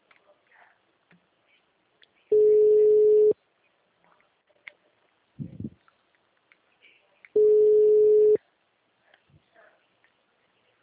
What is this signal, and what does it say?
Ringback tone (Europe, 425 Hz).
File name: out- -2038- - - .62682